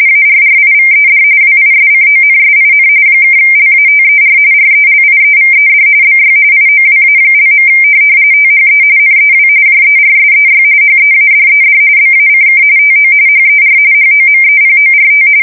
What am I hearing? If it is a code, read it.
RTTY, about 45.45 baud. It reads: ARRIVED WISCONSIN (LATER RELIEVED BY IOWA). THE NEW JERSEY HEADED BACK TO HER HOMEPORT
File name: fsk170 / USS-New-Jersey